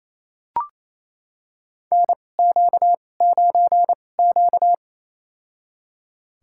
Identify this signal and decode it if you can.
Morse code, heard as a keyed tone, about 28 wpm, spaced as for 20 wpm, 700 Hz. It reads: NQ9Q
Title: USA Call Signs - 4 Characters - 2x Character Spacing - 28wpm